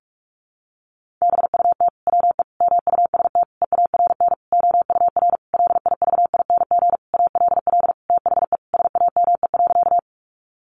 Morse code, heard as a keyed tone, about 45 wpm, 700 Hz. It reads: BUT WE MUST EARN OUR LIVING ALL THE SAME.
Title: Sentences from Top 1500 Words 45wpm